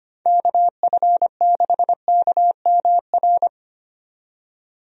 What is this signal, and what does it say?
Morse code, heard as a keyed tone, about 25 wpm, 700 Hz. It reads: KF6KMR
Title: USA Call Signs - 6 Characters - 25wpm